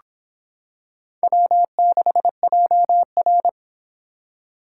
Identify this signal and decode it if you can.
Morse code, heard as a keyed tone, about 24 wpm, 700 Hz. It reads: W6JR